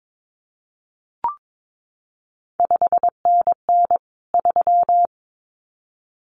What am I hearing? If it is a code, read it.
Morse code, heard as a keyed tone, about 22 wpm, 700 Hz. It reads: 5NN 3
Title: CQ WW DX Contest Exchange Encore - 22wpm